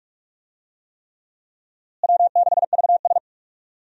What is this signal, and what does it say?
Morse code, heard as a keyed tone, about 41 wpm, 700 Hz. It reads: W6VS
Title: USA Call Signs - 4 Characters - Repeated 3x - 45wpm